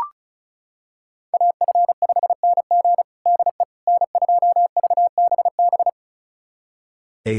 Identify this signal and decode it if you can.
Morse code, heard as a keyed tone, about 36 wpm, 700 Hz. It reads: AF5NG DE N2VBB
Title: USA Callsign DE Callsign - 5 Characters - 35wpm